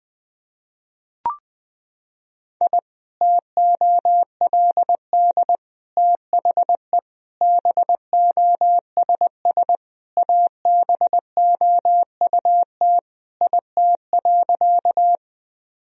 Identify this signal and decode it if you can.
Morse code, heard as a keyed tone, about 20 wpm, 700 Hz. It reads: I TOLD THE BOSS ABOUT IT.
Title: Sentences from Top 1000 Words - Review Part 4 - 20wpm